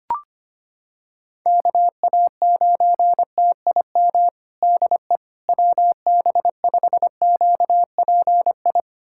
Morse code, heard as a keyed tone, about 25 wpm, 700 Hz. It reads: KA9TIM DE WB5QPI
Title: USA Callsign DE Callsign - 6 Characters - 25wpm